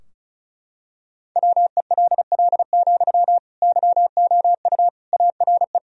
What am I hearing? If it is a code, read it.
Morse code, heard as a keyed tone, about 34 wpm, 700 Hz. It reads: WELL, YOU ARE
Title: Sentences from Top 1000 Words - Review Part 2 - 35wpm